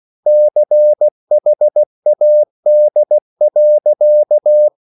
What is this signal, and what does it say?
Morse code, heard as a keyed tone, about 16 wpm, 600 Hz. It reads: CHAD.